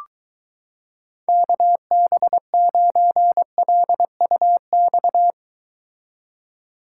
Morse code, heard as a keyed tone, about 23 wpm, 700 Hz. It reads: KB9LUX